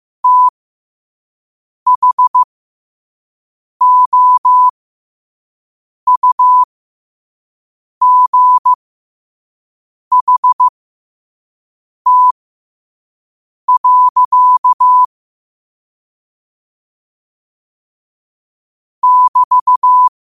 Morse code, heard as a keyed tone, about 15 wpm, spaced as for 5 wpm, 1000 Hz. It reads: THOUGHT. <BT>
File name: 2026-03-16-5wpm